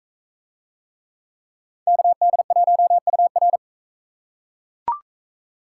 Morse code, heard as a keyed tone, about 39 wpm, 700 Hz. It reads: KD1UR